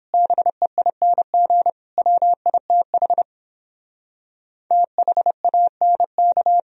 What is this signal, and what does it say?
Morse code, heard as a keyed tone, about 29 wpm, 700 Hz. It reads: BEING WITH THANK